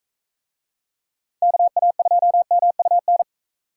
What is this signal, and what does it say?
Morse code, heard as a keyed tone, about 42 wpm, 700 Hz. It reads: KA2MUN